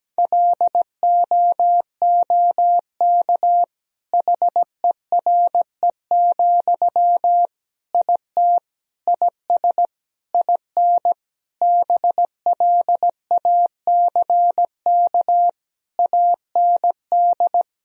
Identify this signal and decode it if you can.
Morse code, heard as a keyed tone, about 17 wpm, 700 Hz. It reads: LOOK HERE, IT IS IN BLACK AND
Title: Sentences from Top 1000 Words - Review Part 2 - 17wpm